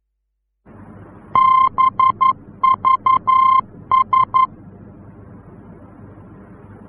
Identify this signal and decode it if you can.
Morse code, heard as a keyed tone, about 11 wpm, 1050 Hz. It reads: BVS
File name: france / mx / BVS-F-VOR